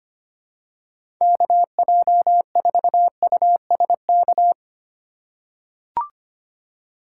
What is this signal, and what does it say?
Morse code, heard as a keyed tone, about 25 wpm, 700 Hz. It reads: KJ4USK
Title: USA Call Signs - 6 Characters - 2x Character Spacing - 25wpm